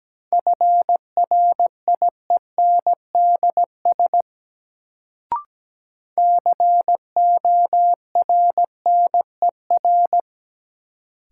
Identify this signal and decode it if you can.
Morse code, heard as a keyed tone, about 17 wpm, 700 Hz. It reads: FRIENDS CORNER
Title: Sentences from Top 1600 Words - New Word Drill Alt - 17wpm